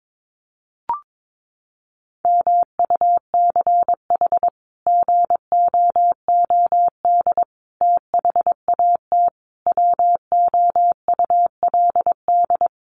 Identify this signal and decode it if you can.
Morse code, heard as a keyed tone, about 22 wpm, 700 Hz. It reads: MUCH GOOD THAT WOULD
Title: Sentences from Top 200 Words 22wpm